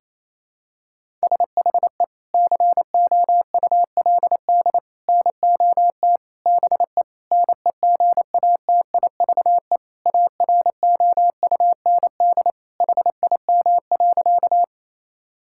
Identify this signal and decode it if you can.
Morse code, heard as a keyed tone, about 27 wpm, 700 Hz. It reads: SHE COULD NOT BE NEGATIVE AROUND HIM.
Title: Sentences from Top 1000 Words - Review Part 6 - 28wpm